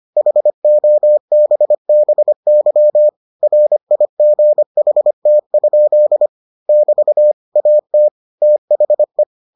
Morse code, heard as a keyed tone, about 25 wpm, 600 Hz. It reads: HOBBY RIGHT? <BT> AT THE